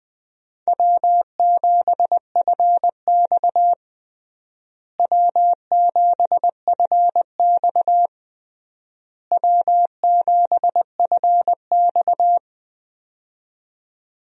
Morse code, heard as a keyed tone, about 19 wpm, 700 Hz. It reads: W7FX W7FX W7FX
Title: USA Call Signs - 4 Characters - Repeated 3x - 20wpm